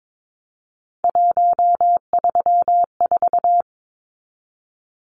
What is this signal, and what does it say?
Morse code, heard as a keyed tone, about 22 wpm, 700 Hz. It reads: 134
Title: Numbers - 3-Digit 22wpm